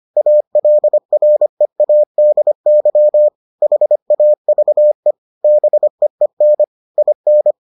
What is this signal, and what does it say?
Morse code, heard as a keyed tone, about 25 wpm, 600 Hz. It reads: ALREADY HAVE BEEN IN